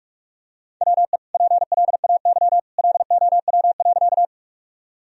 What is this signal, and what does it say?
Morse code, heard as a keyed tone, about 43 wpm, 700 Hz. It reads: WE PLAY LOW.
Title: Sentences from Top 1000 Words - Review Part 1 - 45wpm